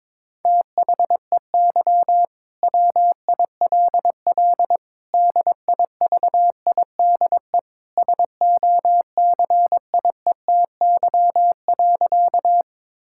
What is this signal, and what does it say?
Morse code, heard as a keyed tone, about 22 wpm, 700 Hz. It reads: THEY WILL DIVIDE SOCIETY.
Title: Sentences from Top 1600 Words 22wpm